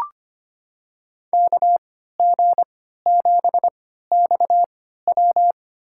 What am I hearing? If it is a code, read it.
Morse code, heard as a keyed tone, about 25 wpm, spaced as for 14 wpm, 700 Hz. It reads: KG7XW
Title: USA Call Signs - 5 Characters - 3x Character Spacing - 25wpm